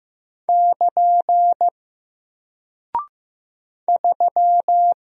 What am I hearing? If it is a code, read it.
Morse code, heard as a keyed tone, about 15 wpm, 700 Hz. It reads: <KN> 3
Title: Single Character (Letter:Number:Punctuation:Prosign) Alt 15wpm